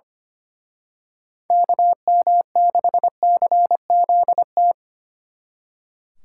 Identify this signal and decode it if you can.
Morse code, heard as a keyed tone, about 31 wpm, 700 Hz. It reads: KM6CZT
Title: Speed Racing - USA 2x3 Call Signs 25wpm